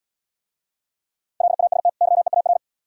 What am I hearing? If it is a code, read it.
Morse code, heard as a keyed tone, about 79 wpm, 700 Hz. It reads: LAST YEAR